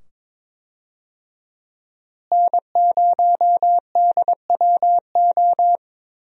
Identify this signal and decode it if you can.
Morse code, heard as a keyed tone, about 22 wpm, 700 Hz. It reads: N0DWO
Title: USA Call Signs - 5 Characters - Repeated 2x - 22wpm